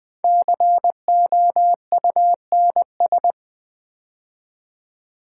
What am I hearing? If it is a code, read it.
Morse code, heard as a keyed tone, about 20 wpm, 700 Hz. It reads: COUNS
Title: Build A Word - 5 Letters At A Time - 10-Letter Words - 20wpm